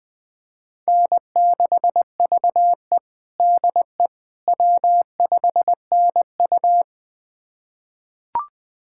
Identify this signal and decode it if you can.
Morse code, heard as a keyed tone, about 20 wpm, 700 Hz. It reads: N6VE DE W5NU
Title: USA Callsign DE Callsign - 4 Characters - 20wpm